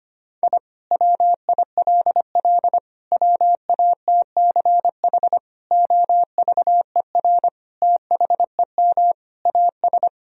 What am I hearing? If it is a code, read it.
Morse code, heard as a keyed tone, about 25 wpm, 700 Hz. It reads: I WILL WATCH OVER THEM AS
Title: Sentences from Top 1000 Words - Review Part 1 - 25wpm